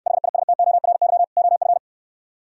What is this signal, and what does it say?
Morse code, heard as a keyed tone, about 68 wpm, 700 Hz. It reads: HISTORY OF